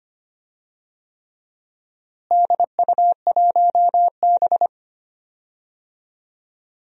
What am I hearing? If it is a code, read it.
Morse code, heard as a keyed tone, about 25 wpm, 700 Hz. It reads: DU1B
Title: DXCC-100 Call Signs - 4 Characters - Repeated 3x - 25wpm